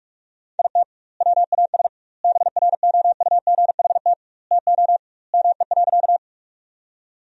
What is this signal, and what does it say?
Morse code, heard as a keyed tone, about 42 wpm, 700 Hz. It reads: IT WAS BROUGHT TO ME.
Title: 2x Word-Spacing - Sentences from Top 500 Words - 45wpm